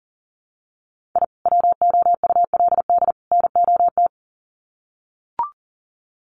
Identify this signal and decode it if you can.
Morse code, heard as a keyed tone, about 40 wpm, 700 Hz. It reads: I WOULD NOT
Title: Sets of 3 Words 40wpm